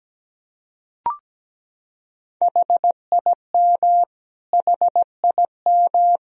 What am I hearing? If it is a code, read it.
Morse code, heard as a keyed tone, about 17 wpm, 700 Hz. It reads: HIM HIM
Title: Top 100 Words Repeated 17wpm